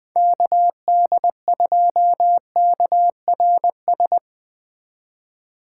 Morse code, heard as a keyed tone, about 20 wpm, 700 Hz. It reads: KD2KRS
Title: Speed Racing - USA 2x3 Call Signs 17wpm